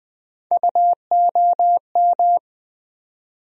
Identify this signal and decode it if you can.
Morse code, heard as a keyed tone, about 21 wpm, 700 Hz. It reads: UOM
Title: USA Call Sign Postfixes - 3 Letters 20wpm